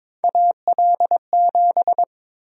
Morse code, heard as a keyed tone, about 22 wpm, 700 Hz. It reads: AL7
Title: USA Call Sign Prefixes - 22wpm